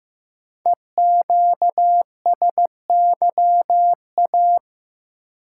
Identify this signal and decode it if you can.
Morse code, heard as a keyed tone, about 15 wpm, 700 Hz. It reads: EQSYA
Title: Code Groups - 5 Letters - 15wpm